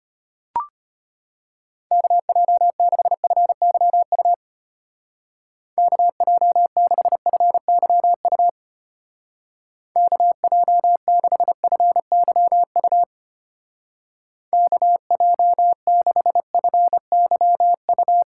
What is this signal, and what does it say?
Morse code, heard as a keyed tone, about 31 wpm, 700 Hz. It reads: KJ6FYU KJ6FYU KJ6FYU KJ6FYU
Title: Speed Racing - USA 2x3 Call Signs 25wpm